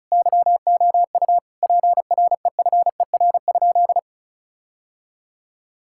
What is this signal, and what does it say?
Morse code, heard as a keyed tone, about 36 wpm, 700 Hz. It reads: YOU PREFER?
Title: Sentences from Top 1600 Words 35wpm